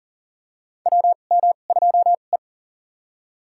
Morse code, heard as a keyed tone, about 40 wpm, spaced as for 29 wpm, 690 Hz. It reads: WM2E
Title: USA Call Signs - 4 Characters - 2x Character Spacing - 40wpm